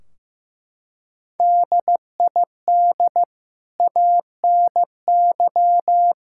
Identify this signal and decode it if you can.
Morse code, heard as a keyed tone, about 15 wpm, 700 Hz. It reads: DID ANY
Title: Sentences from Top 1600 Words 15wpm